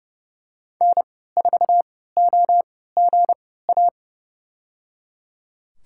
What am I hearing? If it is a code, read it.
Morse code, heard as a keyed tone, about 30 wpm, spaced as for 17 wpm, 700 Hz. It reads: N4OGA
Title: USA Call Signs - 5 Characters - 3x Character Spacing - 30wpm